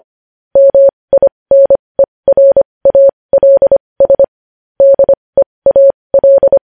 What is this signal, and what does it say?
Morse code, heard as a keyed tone, about 25 wpm, spaced as for 20 wpm, 550 Hz. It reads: MINERALS DEAL